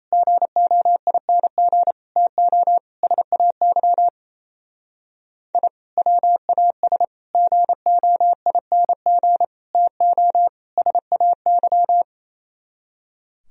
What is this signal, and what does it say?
Morse code, heard as a keyed tone, about 35 wpm, 700 Hz. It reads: GOING TO SAY I WAS GOING TO SAY